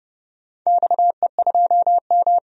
Morse code, heard as a keyed tone, about 29 wpm, 700 Hz. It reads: XE2M
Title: DXCC-100 Call Signs - 4 Characters - 30wpm